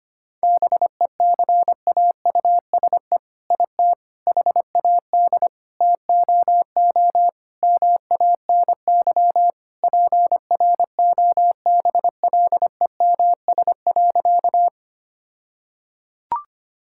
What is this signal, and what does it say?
Morse code, heard as a keyed tone, about 25 wpm, 700 Hz. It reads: BECAUSE IT HAD TOO MANY PROBLEMS.
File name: practice / 25wpm / All-Age Appropriate Jokes - 25wpm